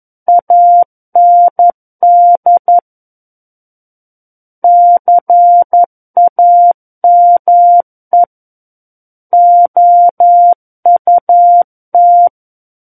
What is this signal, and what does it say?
Morse code, heard as a keyed tone, about 11 wpm, 700 Hz. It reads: AND CAME OUT
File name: 11 WPM